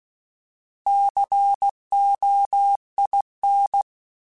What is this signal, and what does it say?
Morse code, heard as a keyed tone, about 16 wpm, 800 Hz. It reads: COIN
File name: Alphabetical / Alphabetical mp3s / Cs @15